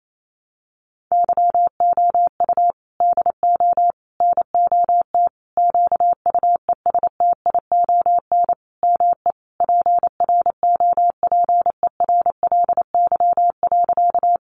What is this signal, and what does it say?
Morse code, heard as a keyed tone, about 28 wpm, 700 Hz. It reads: YOU DO NOT QUESTION ME PROPERLY.
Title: Sentences from Top 1100 Words 28wpm